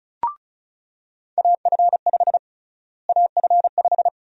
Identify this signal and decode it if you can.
Morse code, heard as a keyed tone, about 36 wpm, 700 Hz. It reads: AF5 AF5
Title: USA Call Sign Prefixes - Repeated 2x - 35wpm